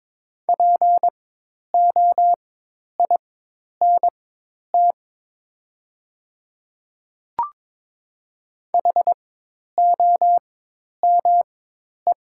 Morse code, heard as a keyed tone, about 22 wpm, spaced as for 10 wpm, 700 Hz. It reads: POINT HOME
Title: 4x Character Spacing - Top 200 Words - 22wpm